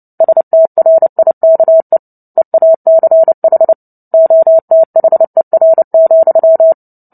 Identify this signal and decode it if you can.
Morse code, heard as a keyed tone, about 29 wpm, 650 Hz. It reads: STRIKE EACH OTHER,
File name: Book Two / 03-Chapter 3 - 29 WPM